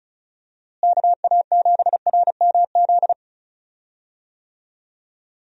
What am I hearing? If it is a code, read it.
Morse code, heard as a keyed tone, about 34 wpm, 700 Hz. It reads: KA7RMZ
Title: USA Call Signs - 6 Characters - Repeated 2x - 35wpm